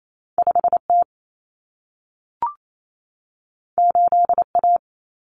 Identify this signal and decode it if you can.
Morse code, heard as a keyed tone, about 28 wpm, 700 Hz. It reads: 5T 8A
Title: Common Cut Numbers - 2-Digit 28wpm